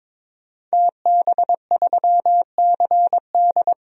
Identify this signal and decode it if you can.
Morse code, heard as a keyed tone, about 22 wpm, 700 Hz. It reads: TB3CD
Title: DXCC-100 Call Signs - 5 Characters - 22wpm